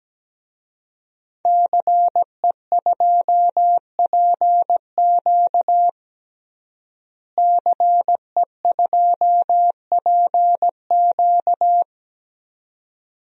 Morse code, heard as a keyed tone, about 17 wpm, 700 Hz. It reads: CE2PQ CE2PQ
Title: DXCC-100 Call Signs - 5 Characters - Repeated 2x - 17wpm